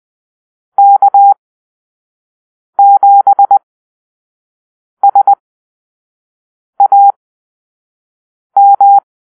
Morse code, heard as a keyed tone, about 20 wpm, spaced as for 5 wpm, 800 Hz. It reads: K7SAM
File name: arrl / Call-Sign Drill-4